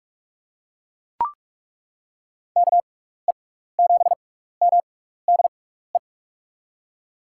Morse code, heard as a keyed tone, about 41 wpm, spaced as for 15 wpm, 700 Hz. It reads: KE7MDE
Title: USA Call Signs - 6 Characters - 6x Character Spacing - 45wpm